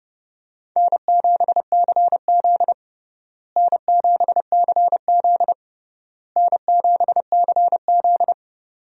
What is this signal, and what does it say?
Morse code, heard as a keyed tone, about 30 wpm, 700 Hz. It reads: N7CZ N7CZ N7CZ
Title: USA Call Signs - 4 Characters - Repeated 3x - 30wpm